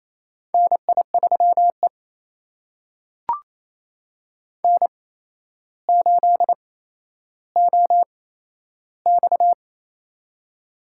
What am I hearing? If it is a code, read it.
Morse code, heard as a keyed tone, about 28 wpm, spaced as for 12 wpm, 700 Hz. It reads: NI3E N8OX
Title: USA Call Signs - 4 Characters - 8x Character Spacing - 28wpm